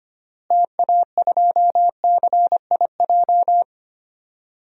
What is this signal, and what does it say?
Morse code, heard as a keyed tone, about 25 wpm, 700 Hz. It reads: TA2CIJ
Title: DXCC-100 Call Signs - 6 Characters - Repeated 3x - 25wpm